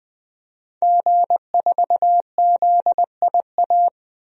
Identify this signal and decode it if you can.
Morse code, heard as a keyed tone, about 20 wpm, 700 Hz. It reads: G4ZIA
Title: DXCC-100 Call Signs - 5 Characters - Repeated 3x - 20wpm